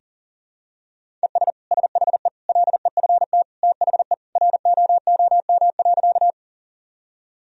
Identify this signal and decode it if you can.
Morse code, heard as a keyed tone, about 37 wpm, 700 Hz. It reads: ES SHE LEFT THE ROOM.
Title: Sentences from Top 300 Words and Abbreviations 40wpm